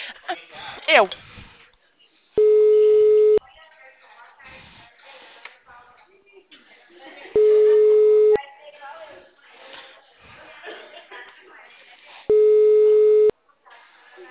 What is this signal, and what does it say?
Ringback tone (Europe, 425 Hz), with speech.